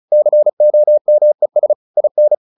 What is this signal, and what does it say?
Morse code, heard as a keyed tone, about 34 wpm, 600 Hz. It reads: COMES IN
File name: the old man and the sea 2 35wpm